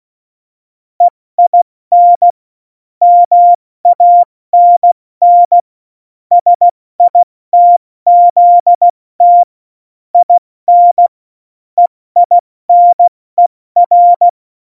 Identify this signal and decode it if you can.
Morse code, heard as a keyed tone, about 16 wpm, 700 Hz. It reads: EIN MANN SITZT IN EINER
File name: Morsekurs / Witze / Witz 01. Oktober 2019, 70 F80